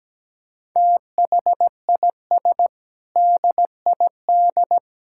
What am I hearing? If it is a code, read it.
Morse code, heard as a keyed tone, about 17 wpm, 700 Hz. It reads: THIS DID
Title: Sentences from Top 1000 Words - Review Part 6 - 17wpm